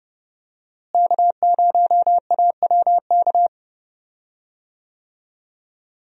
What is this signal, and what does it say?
Morse code, heard as a keyed tone, about 30 wpm, 700 Hz. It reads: K0AWK
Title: USA Call Signs - 5 Characters - Repeated 3x - 30wpm